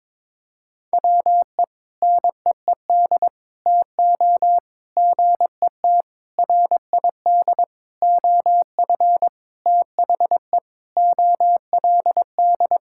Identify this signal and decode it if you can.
Morse code, heard as a keyed tone, about 22 wpm, 700 Hz. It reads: WE NEED TO GET RID OF THE OLD
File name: practice / 22wpm / Sentences with 4-Word Phrases - 22wpm